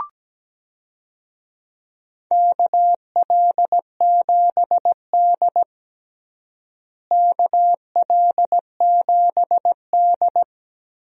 Morse code, heard as a keyed tone, about 17 wpm, 700 Hz. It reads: KL7D KL7D
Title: DXCC-100 Call Signs - 4 Characters - Repeated 3x - 17wpm